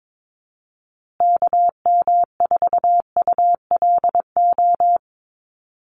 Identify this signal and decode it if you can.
Morse code, heard as a keyed tone, about 22 wpm, 700 Hz. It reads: KM4ULO
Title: USA Call Signs Repeated 3x 22wpm